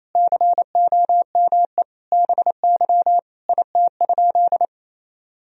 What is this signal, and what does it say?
Morse code, heard as a keyed tone, about 28 wpm, 700 Hz. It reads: COME BY IT?
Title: Sentences from Top 100 Words and Abbreviations 28wpm